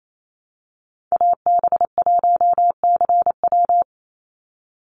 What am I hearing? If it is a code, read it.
Morse code, heard as a keyed tone, about 28 wpm, 700 Hz. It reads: AB1CW
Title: Call Signs Repeated 2x - 28wpm